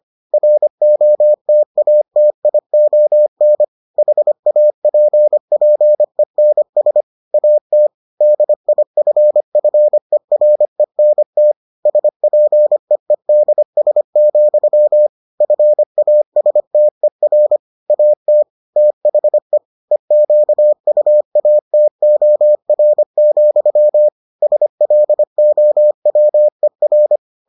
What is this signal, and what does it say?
Morse code, heard as a keyed tone, about 25 wpm, 600 Hz. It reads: ROTATION HAPPENS AT DIFFERENT SPEEDS, FASTER AT THE EQUATOR, SLOWER